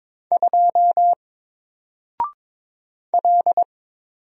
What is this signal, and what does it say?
Morse code, heard as a keyed tone, about 22 wpm, 700 Hz. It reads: 2 L
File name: Single-Letter-Number-22wpm